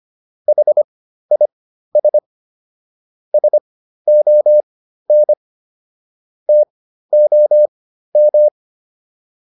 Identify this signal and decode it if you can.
Morse code, heard as a keyed tone, about 25 wpm, spaced as for 13 wpm, 600 Hz. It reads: HIS SON TOM